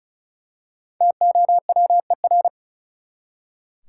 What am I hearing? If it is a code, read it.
Morse code, heard as a keyed tone, about 36 wpm, 700 Hz. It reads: TOWER
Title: Five Letter Words - 35wpm